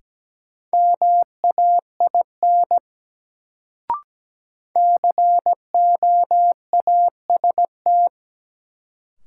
Morse code, heard as a keyed tone, about 17 wpm, 700 Hz. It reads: MAIN COAST
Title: Sentences from Top 1300 Words - New Word Drill Alt - 17wpm